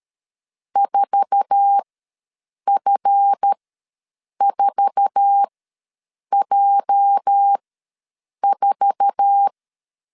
Morse code, heard as a keyed tone, about 13 wpm, spaced as for 7 wpm, 800 Hz. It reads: 4F4J4
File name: LEZIONI / 8 YJ45 / 8 YJ4 ws 12wpm 3249